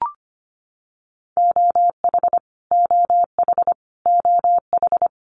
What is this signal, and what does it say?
Morse code, heard as a keyed tone, about 25 wpm, 700 Hz. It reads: OH OH OH
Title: US States Repeated 3x 25wpm